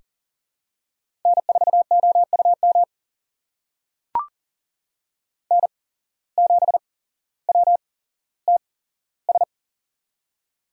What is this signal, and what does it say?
Morse code, heard as a keyed tone, about 40 wpm, spaced as for 17 wpm, 700 Hz. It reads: N4OUM N7WTS
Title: USA Call Signs - 5 Characters - 8x Character Spacing - 40wpm